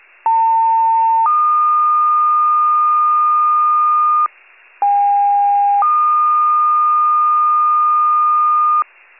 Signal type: two-tone page

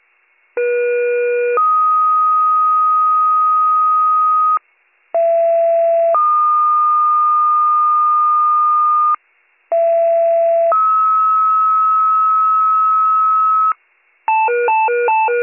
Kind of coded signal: two-tone page

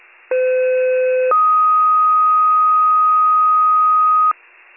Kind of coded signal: two-tone page

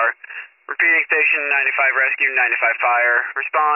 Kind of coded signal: Emergency Alert System (SAME) tones

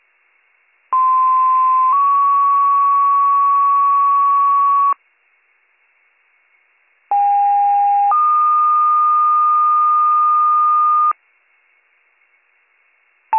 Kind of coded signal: two-tone page

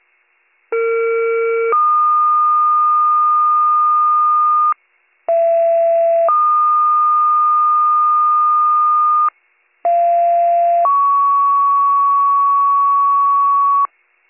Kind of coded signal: two-tone page

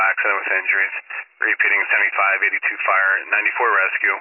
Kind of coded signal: Emergency Alert System (SAME) tones

